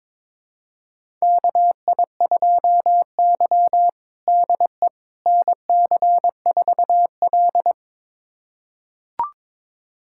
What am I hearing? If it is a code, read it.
Morse code, heard as a keyed tone, about 22 wpm, 700 Hz. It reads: KI2Y DE NC4L
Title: USA Callsign DE Callsign - 4 Characters - 22wpm